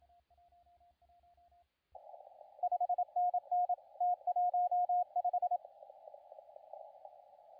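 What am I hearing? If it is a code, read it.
Morse code, heard as a keyed tone, about 27 wpm, 720 Hz, with radio noise. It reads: 5NN T15
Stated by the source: CW contest audio recorded at LA4C during the SAC CW contest.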